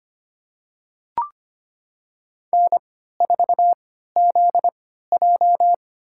Morse code, heard as a keyed tone, about 25 wpm, spaced as for 14 wpm, 700 Hz. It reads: N4ZJ
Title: USA Call Signs - 3x Character Spacing - 25wpm